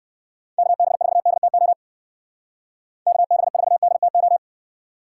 Morse code, heard as a keyed tone, about 59 wpm, 700 Hz. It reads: KB4DTO KB4DTO